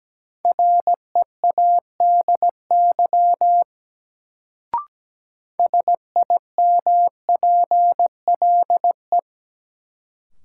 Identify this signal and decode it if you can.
Morse code, heard as a keyed tone, about 17 wpm, 700 Hz. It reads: READY SIMPLE
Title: Sentences from Top 400 Words - New Word Drill - 17wpm